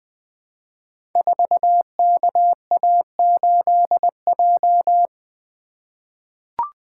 Morse code, heard as a keyed tone, about 20 wpm, 700 Hz. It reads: 4KA8J
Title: Code Groups - 5 Letter:Numbers - 20wpm